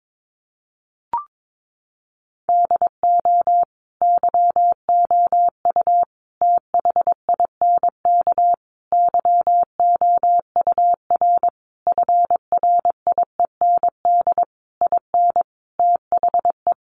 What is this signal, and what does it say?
Morse code, heard as a keyed tone, about 22 wpm, 700 Hz. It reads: DO YOU THINK YOUR FRIEND IN THE
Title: Sentences from Top 1100 Words 22wpm